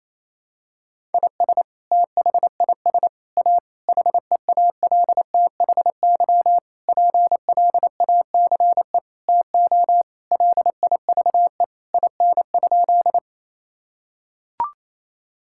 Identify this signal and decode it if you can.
Morse code, heard as a keyed tone, about 27 wpm, 700 Hz. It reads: IS THIS A HEALTHY PLACE TO LIVE IN?
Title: Sentences from Top 1000 Words - Review Part 5 - 28wpm